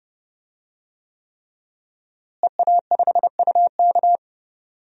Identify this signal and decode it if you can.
Morse code, heard as a keyed tone, about 32 wpm, 700 Hz. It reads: EA5UK
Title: DXCC-100 Call Signs - 5 Characters - Repeated 3x - 30wpm